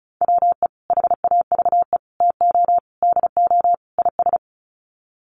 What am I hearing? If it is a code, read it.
Morse code, heard as a keyed tone, about 34 wpm, 700 Hz. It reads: WE HAVE TO DO IS
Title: Sets of 5 Words Encore 35wpm